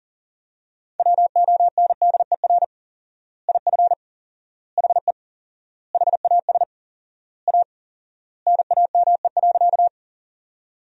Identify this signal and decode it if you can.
Morse code, heard as a keyed tone, about 40 wpm, 700 Hz. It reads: WONDER IF HE HAS A NAME.
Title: 4x Word-Spacing - Sentences from Top 500 Words - 40wpm